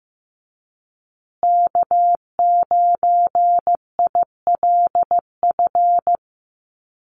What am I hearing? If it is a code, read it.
Morse code, heard as a keyed tone, about 15 wpm, 700 Hz. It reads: K9ILF
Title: USA Call Signs Repeated 3x 15wpm